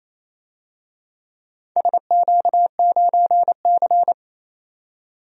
Morse code, heard as a keyed tone, about 27 wpm, 700 Hz. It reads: SQ9C
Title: DXCC-100 Call Signs - 4 Characters - 28wpm